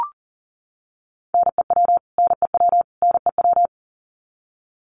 Morse code, heard as a keyed tone, about 40 wpm, 700 Hz. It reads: NEW NEW NEW
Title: Top 500 Words Repeated 40wpm